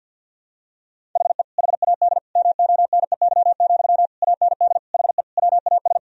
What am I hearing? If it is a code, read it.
Morse code, heard as a keyed tone, about 50 wpm, 700 Hz. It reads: HE HAD MONEY, AND HE WAS
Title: Sentences from Top 1000 Words - Review Part 3 - 50wpm